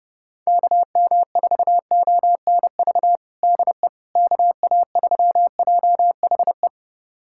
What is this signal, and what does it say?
Morse code, heard as a keyed tone, about 32 wpm, 700 Hz. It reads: KM4ONV DE KA3JHE
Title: USA Callsign DE Callsign - 6 Characters - 30wpm